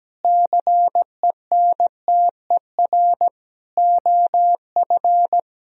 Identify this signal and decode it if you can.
Morse code, heard as a keyed tone, about 17 wpm, 700 Hz. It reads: CENTER OF
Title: Sentences from Top 1000 Words - Review Part 2 - 17wpm